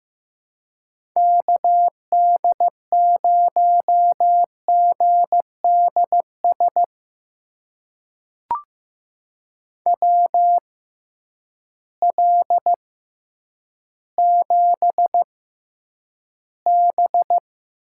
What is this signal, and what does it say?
Morse code, heard as a keyed tone, about 15 wpm, spaced as for 9 wpm, 700 Hz. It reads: KD0GDS WL7B
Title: USA Call Signs - 6 Characters - 6x Character Spacing - 15wpm